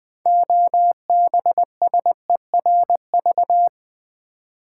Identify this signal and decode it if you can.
Morse code, heard as a keyed tone, about 20 wpm, 700 Hz. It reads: OBSERV
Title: Build A Word - 2 Letters At A Time - 8-Letter Words - 20wpm